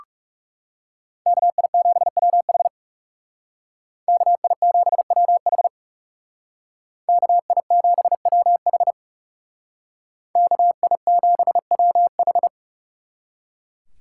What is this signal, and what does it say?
Morse code, heard as a keyed tone, about 38 wpm, 700 Hz. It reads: KI7WH KI7WH KI7WH KI7WH